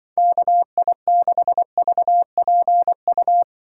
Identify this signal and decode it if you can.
Morse code, heard as a keyed tone, about 24 wpm, 700 Hz. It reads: KI6VPU
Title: Speed Racing - USA 2x3 Call Signs 20wpm